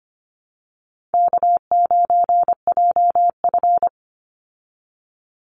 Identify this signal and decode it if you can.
Morse code, heard as a keyed tone, about 25 wpm, 700 Hz. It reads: K9JF
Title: Easy Call Signs Repeated 2x - 25wpm